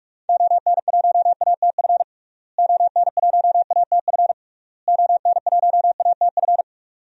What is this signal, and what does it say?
Morse code, heard as a keyed tone, about 46 wpm, 700 Hz. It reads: ON1ATF ON1ATF ON1ATF
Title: DXCC-100 Call Signs - 6 Characters - Repeated 3x - 45wpm